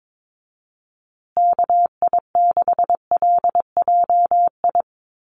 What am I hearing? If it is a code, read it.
Morse code, heard as a keyed tone, about 22 wpm, 700 Hz. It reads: KI6LJI
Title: USA Call Signs Repeated 2x 22wpm